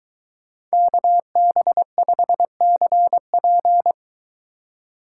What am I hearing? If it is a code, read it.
Morse code, heard as a keyed tone, about 23 wpm, 700 Hz. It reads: KB5CP